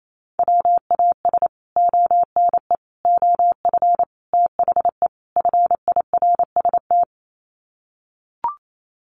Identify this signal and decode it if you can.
Morse code, heard as a keyed tone, about 28 wpm, 700 Hz. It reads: WAS ONE OF THE FIRST
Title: Sets of 5 Words 28wpm